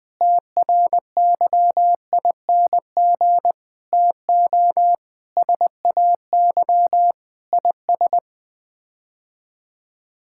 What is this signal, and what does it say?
Morse code, heard as a keyed tone, about 20 wpm, 700 Hz. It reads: TRYING TO SAY IS
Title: Sets of 8 Words - 20wpm